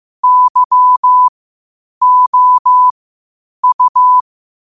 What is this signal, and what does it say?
Morse code, heard as a keyed tone, about 15 wpm, spaced as for 8 wpm, 1000 Hz. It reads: YOU
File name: qotdfiles / 8wpm